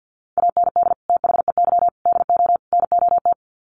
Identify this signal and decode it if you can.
Morse code, heard as a keyed tone, about 50 wpm, 700 Hz. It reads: AND THEY DO NOT
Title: Sets of 4 Words 50wpm